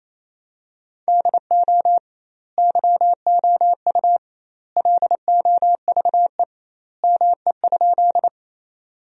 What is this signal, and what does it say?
Morse code, heard as a keyed tone, about 27 wpm, 700 Hz. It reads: DO YOU LOVE ME?
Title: 2x Word-Spacing - Sentences from Top 400 Words - 28wpm